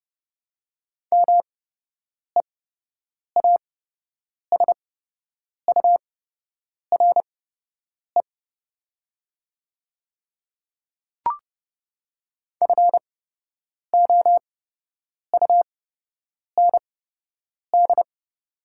Morse code, heard as a keyed tone, about 30 wpm, spaced as for 8 wpm, 700 Hz. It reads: MEASURE FOUND
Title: 8x Character Spacing - Top 500 Words - 30wpm